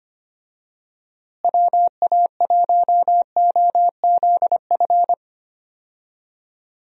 Morse code, heard as a keyed tone, about 25 wpm, 700 Hz. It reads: WA1OZF
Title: USA Call Signs - 6 Characters - Repeated 3x - 25wpm